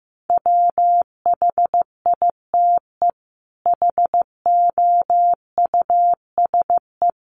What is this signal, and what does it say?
Morse code, heard as a keyed tone, about 15 wpm, 700 Hz. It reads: WHITE HOUSE
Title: Sets of 5 Words Encore 15wpm